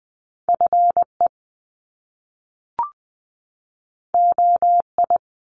Morse code, heard as a keyed tone, about 20 wpm, 700 Hz. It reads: F E O I
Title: Top Two Letter Combinations in Words 20wpm